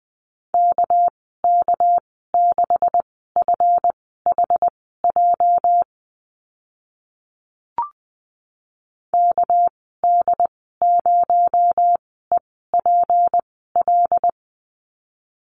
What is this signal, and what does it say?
Morse code, heard as a keyed tone, about 20 wpm, spaced as for 14 wpm, 700 Hz. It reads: KK6FHJ KD0EPL
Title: 245 Lesson 39g - Letter z - Call Signs 20wpm